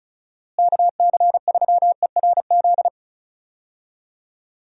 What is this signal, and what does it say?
Morse code, heard as a keyed tone, about 36 wpm, 700 Hz. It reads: KC3ERZ
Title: USA Call Signs - 6 Characters - Repeated 3x - 35wpm